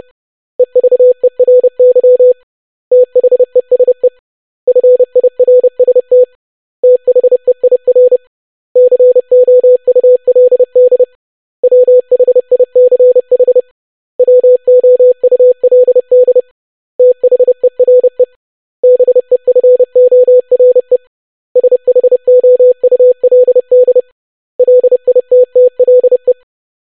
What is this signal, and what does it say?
Morse code, heard as a keyed tone, about 30 wpm, 500 Hz. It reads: EVERY THESE FIRST THEIR COULD WHICH WOULD THERE BEFORE SHOULD LITTLE
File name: basic100-25wpm